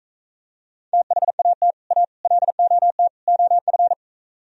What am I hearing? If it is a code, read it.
Morse code, heard as a keyed tone, about 43 wpm, 700 Hz. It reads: THAT A LOT OF